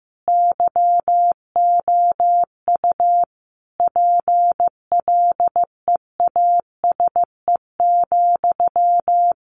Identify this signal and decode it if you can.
Morse code, heard as a keyed tone, about 15 wpm, 700 Hz. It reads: YOU PLEASE,
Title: Sentences from Top 1500 Words 15wpm